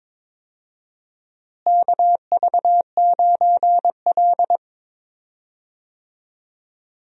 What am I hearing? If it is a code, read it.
Morse code, heard as a keyed tone, about 21 wpm, 700 Hz. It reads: KV9L
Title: USA Call Signs - 4 Characters - Repeated 2x - 22wpm